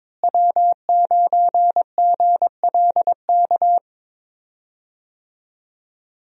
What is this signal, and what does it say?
Morse code, heard as a keyed tone, about 22 wpm, 700 Hz. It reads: W9GLK
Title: USA Call Signs - 5 Characters - Repeated 2x - 22wpm